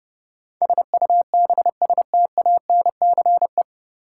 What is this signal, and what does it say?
Morse code, heard as a keyed tone, about 30 wpm, 700 Hz. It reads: SUBSTANCE
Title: Sentences from Top 1500 Words - New Word Drill Alt - 30wpm